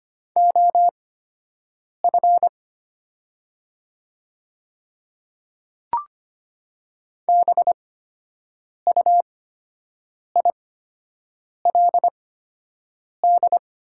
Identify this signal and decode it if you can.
Morse code, heard as a keyed tone, about 25 wpm, spaced as for 7 wpm, 700 Hz. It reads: OF BUILD
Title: 8x Character Spacing - Top 300 Words - 25wpm